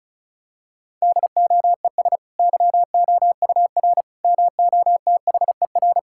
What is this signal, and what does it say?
Morse code, heard as a keyed tone, about 36 wpm, 700 Hz. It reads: DOES YOUR MOTHER
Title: Sentences from Top 1000 Words - Review Part 5 - 35wpm